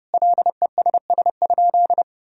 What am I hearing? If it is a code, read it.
Morse code, heard as a keyed tone, about 30 wpm, 700 Hz. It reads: LESS?
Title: Sentences from Top 1000 Words - Review Part 2 - 30wpm